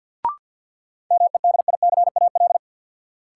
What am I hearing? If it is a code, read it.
Morse code, heard as a keyed tone, about 53 wpm, 700 Hz. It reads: MEDICAL
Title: Sentences from Top 800 Words - New Word Drill - 50wpm